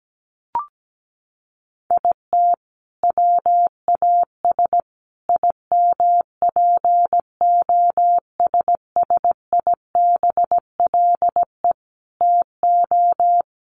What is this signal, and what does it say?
Morse code, heard as a keyed tone, about 17 wpm, 700 Hz. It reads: IT WAS IMPOSSIBLE TO
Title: Sentences from Top 1400 Words 17wpm